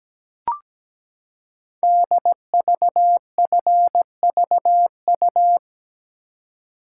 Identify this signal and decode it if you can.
Morse code, heard as a keyed tone, about 17 wpm, 700 Hz. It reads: DVFVU
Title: Code Groups - 5 Letters - 17wpm